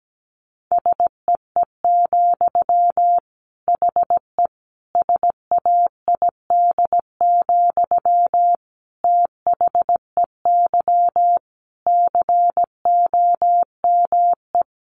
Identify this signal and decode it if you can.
Morse code, heard as a keyed tone, about 17 wpm, 700 Hz. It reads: SEE, HE SAID, THEY COME
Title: Sentences from Top 700 Words 17wpm